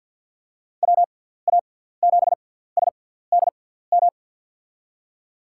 Morse code, heard as a keyed tone, about 52 wpm, spaced as for 17 wpm, 700 Hz. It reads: WA7SDM
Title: USA Call Signs - 6 Characters - 6x Character Spacing - 50wpm